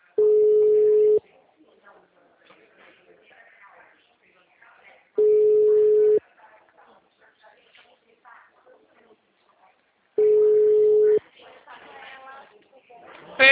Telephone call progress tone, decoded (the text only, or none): ringback tone